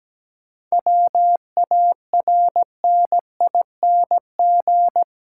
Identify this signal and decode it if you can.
Morse code, heard as a keyed tone, about 17 wpm, 700 Hz. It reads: WARNING
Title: Seven Letter Words - 17wpm